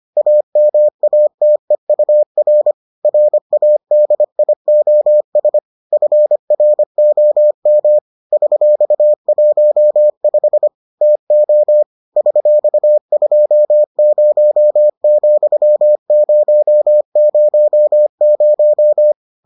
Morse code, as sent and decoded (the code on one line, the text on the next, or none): .- -- .- - . ..- .-. / .-. .- -.. .. --- ... / ..-. .-. --- -- / ...-..- .---- ..... / - --- / ...-..- ..--- ----- --..-- ----- ----- -----
AMATEUR RADIOS FROM $15 TO $20,000